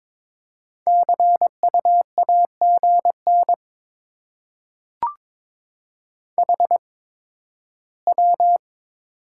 Morse code, heard as a keyed tone, about 22 wpm, spaced as for 13 wpm, 700 Hz. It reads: CU AGN HW
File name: Top 20 QSO Elements - 8x Character Spacing - 22wpm